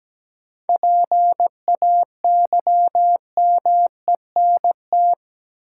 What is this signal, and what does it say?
Morse code, heard as a keyed tone, about 17 wpm, 700 Hz. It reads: PAYMENT
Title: Sentences from Top 1200 Words - New Word Drill Alt - 17wpm